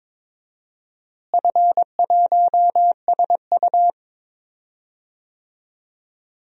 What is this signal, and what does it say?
Morse code, heard as a keyed tone, about 21 wpm, 700 Hz. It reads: F1SU
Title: DXCC-100 Call Signs - 4 Characters - Repeated 2x - 22wpm